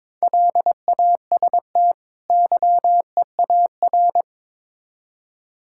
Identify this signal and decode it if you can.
Morse code, heard as a keyed tone, about 22 wpm, 700 Hz. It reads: LAST YEAR
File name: Build A Phrase - 2-Word Phrases - 22wpm